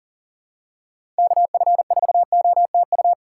Morse code, heard as a keyed tone, about 40 wpm, 700 Hz. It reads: KF4OTU
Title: USA Call Signs - 6 Characters - 2x Character Spacing - 40wpm